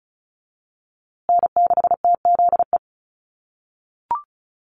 Morse code, heard as a keyed tone, about 34 wpm, 700 Hz. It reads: N6TZE
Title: USA Call Signs Repeated 2x 35wpm